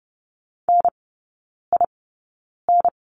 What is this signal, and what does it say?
Morse code, heard as a keyed tone, about 30 wpm, spaced as for 9 wpm, 690 Hz. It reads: NIN
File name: Lesson 16e - Letter c - Three Characters 30wpm